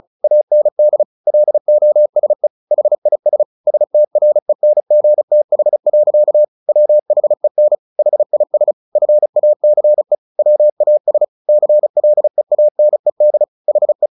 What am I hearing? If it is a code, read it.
Morse code, heard as a keyed tone, about 34 wpm, 600 Hz. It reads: AND LOSE HIS STRENGTH. WHEN HIS FACE WAS CLEANED HE
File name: the old man and the sea 2 35wpm